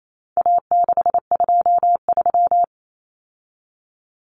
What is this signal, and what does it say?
Morse code, heard as a keyed tone, about 28 wpm, 700 Hz. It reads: A623
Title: Common Cut Numbers - 4-Digit 28wpm